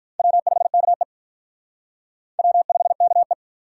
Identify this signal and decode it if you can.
Morse code, heard as a keyed tone, about 45 wpm, 700 Hz. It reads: W5KE W5KE